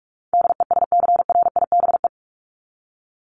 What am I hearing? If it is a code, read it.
Morse code, heard as a keyed tone, about 44 wpm, 700 Hz. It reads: DESCRIBE
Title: ED Suffix - Top 100 Words with Base Words 45wpm